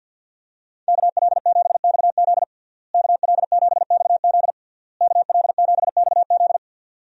Morse code, heard as a keyed tone, about 52 wpm, 700 Hz. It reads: KL7XZ KL7XZ KL7XZ
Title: DXCC-100 Call Signs - 5 Characters - Repeated 3x - 50wpm